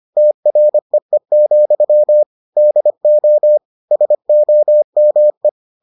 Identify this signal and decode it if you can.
Morse code, heard as a keyed tone, about 25 wpm, 600 Hz. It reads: TREE, DO SOME